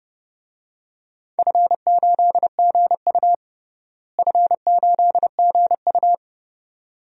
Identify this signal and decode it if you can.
Morse code, heard as a keyed tone, about 30 wpm, 700 Hz. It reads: F8GU F8GU
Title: DXCC-100 Call Signs - 4 Characters - Repeated 2x - 30wpm